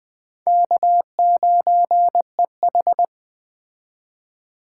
Morse code, heard as a keyed tone, about 21 wpm, 700 Hz. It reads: K9EH
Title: Speed Racing - USA 1x2 Call Signs 17wpm